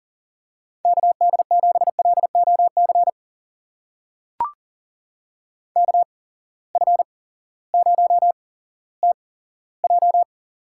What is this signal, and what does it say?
Morse code, heard as a keyed tone, about 40 wpm, spaced as for 18 wpm, 700 Hz. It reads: KD7LOC KF0TJ
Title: USA Call Signs - 8x Character Spacing - 40wpm